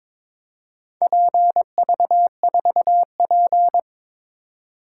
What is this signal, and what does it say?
Morse code, heard as a keyed tone, about 22 wpm, 700 Hz. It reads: PV4P
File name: practice / 22wpm / Four Letter-Number Combinations - 22wpm